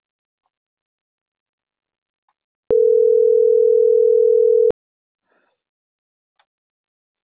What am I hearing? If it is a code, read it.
Ringback tone (North America), 440 + 480 Hz.